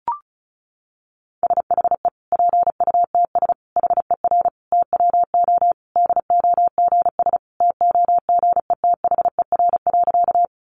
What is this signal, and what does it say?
Morse code, heard as a keyed tone, about 34 wpm, 700 Hz. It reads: SHE PUTS HER TWO DOGS TOGETHER.
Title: Easy Phrases 35wpm